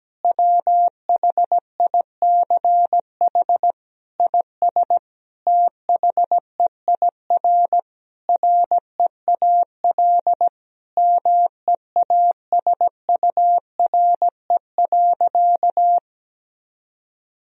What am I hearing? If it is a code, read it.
Morse code, heard as a keyed tone, about 17 wpm, 700 Hz. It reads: WHICH IS THEIR REAL MEASURE.
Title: Sentences from Top 1000 Words - Review Part 1 - 17wpm